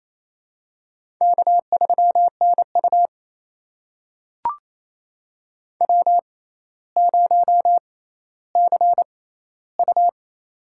Morse code, heard as a keyed tone, about 28 wpm, spaced as for 14 wpm, 700 Hz. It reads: K3NU W0CU
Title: USA Call Signs - 4 Characters - 6x Character Spacing - 28wpm